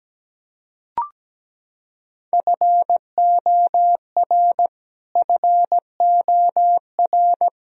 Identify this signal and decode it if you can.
Morse code, heard as a keyed tone, about 17 wpm, 700 Hz. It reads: FOR FOR
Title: Top 100 Words Repeated 17wpm